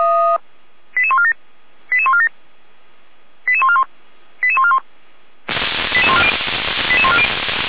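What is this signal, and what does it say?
Five-tone selective call (ZVEI), heard as quick runs of five tones, tones of about 70 ms. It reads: 88147 88147 88141 88141 88140
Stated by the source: ZVEI five-tone call 88501 received by SDR, fire brigade Feuerwehr Orlamünde (f85360).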